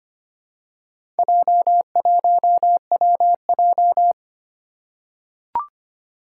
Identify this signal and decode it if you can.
Morse code, heard as a keyed tone, about 25 wpm, 700 Hz. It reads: J1WJ
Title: Four Challenging Letter-Number Combinations 25wpm